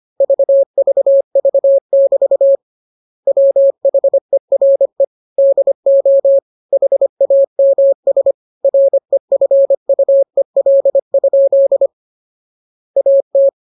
Morse code, heard as a keyed tone, about 25 wpm, 550 Hz. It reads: VVV<BT> WHERE DO HAMS REFUEL? AT